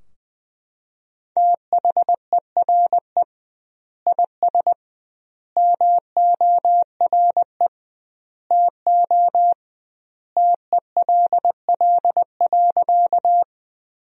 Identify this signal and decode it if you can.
Morse code, heard as a keyed tone, about 20 wpm, 700 Hz. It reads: THERE IS MORE TO TELL.
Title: 2x Word-Spacing - Sentences from Top 200 Words - 20wpm